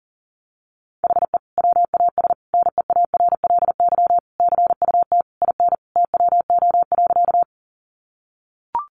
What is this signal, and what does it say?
Morse code, heard as a keyed tone, about 40 wpm, 700 Hz. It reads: HE WAS NEARLY CUT IN TWO.
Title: Sentences from Top 1300 Words 40wpm